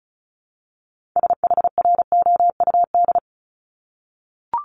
Morse code, heard as a keyed tone, about 36 wpm, 700 Hz. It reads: SHROUD
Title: Multi-letter Phonemes Repeated 3x 35wpm